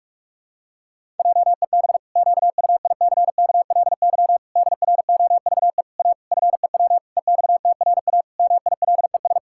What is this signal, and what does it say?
Morse code, heard as a keyed tone, about 41 wpm, 700 Hz. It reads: JEB QUICKLY DROVE A FEW EXTRA MILES
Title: Pangram Challenge v2 - 45wpm